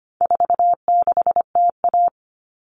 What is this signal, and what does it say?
Morse code, heard as a keyed tone, about 25 wpm, 700 Hz. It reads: 46TA
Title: All Cut Numbers - 4-Digit 25wpm